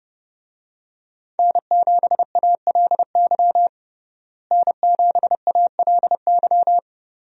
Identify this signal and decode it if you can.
Morse code, heard as a keyed tone, about 30 wpm, 700 Hz. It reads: N7ALY N7ALY
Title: USA Call Signs - 5 Characters - Repeated 2x - 30wpm